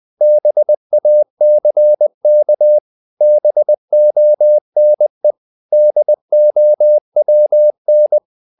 Morse code, heard as a keyed tone, about 20 wpm, 600 Hz. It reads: BACK BONE DOWN
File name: the old man and the sea / the old man and the sea part 1 20wpm